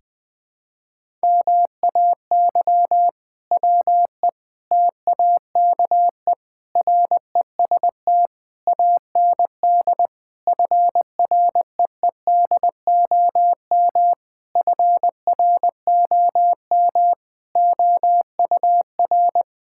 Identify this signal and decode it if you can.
Morse code, heard as a keyed tone, about 19 wpm, 700 Hz. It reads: MAY WE TAKE REST AND FREEDOM FROM OUR
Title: Sentences from Top 1600 Words 20wpm